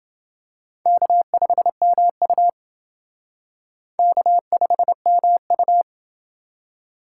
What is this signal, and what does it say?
Morse code, heard as a keyed tone, about 28 wpm, 700 Hz. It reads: K5MU K5MU